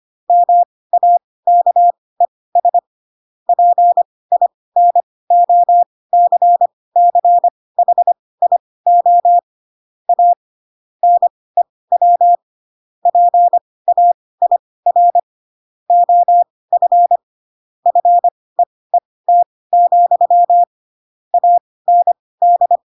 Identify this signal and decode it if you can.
Morse code, heard as a keyed tone, about 25 wpm, spaced as for 18 wpm, 700 Hz. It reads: MAKES PINOCCHIO A NEW PAIR OF FEET, AND